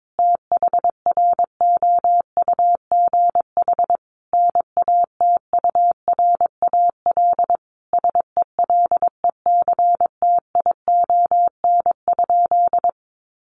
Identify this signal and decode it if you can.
Morse code, heard as a keyed tone, about 22 wpm, 700 Hz. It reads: THROUGH NATURAL SELECTION?
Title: Sentences from Top 1400 Words 22wpm